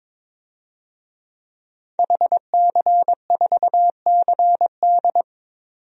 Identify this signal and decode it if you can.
Morse code, heard as a keyed tone, about 22 wpm, 700 Hz. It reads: HC4CD
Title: DXCC-100 Call Signs - 5 Characters - 22wpm